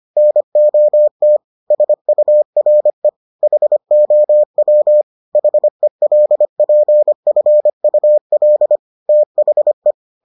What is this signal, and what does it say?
Morse code, heard as a keyed tone, about 25 wpm, 600 Hz. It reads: NOT SURE HOW HELPFUL THE